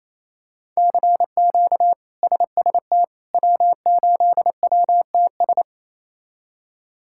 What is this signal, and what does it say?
Morse code, heard as a keyed tone, about 27 wpm, 700 Hz. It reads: CQ SST W8WTS
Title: K1USN SST QSO Practice - 28wpm